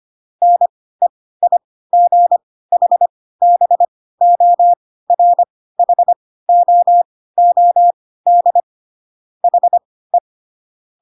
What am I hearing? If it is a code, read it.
Morse code, heard as a keyed tone, about 25 wpm, spaced as for 16 wpm, 700 Hz. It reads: NEIGHBORHOOD HE
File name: Pinocchio / Pinocchio ch2 10 25 ik1hge